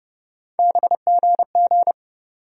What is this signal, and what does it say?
Morse code, heard as a keyed tone, about 30 wpm, 700 Hz. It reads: BGG
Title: USA Call Sign Postfixes - 3 Letters 30wpm